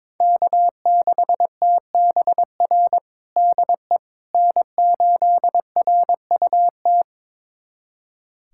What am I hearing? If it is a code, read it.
Morse code, heard as a keyed tone, about 22 wpm, 700 Hz. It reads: K6TBR DE N8RUT
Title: USA Callsign DE Callsign - 5 Characters - 22wpm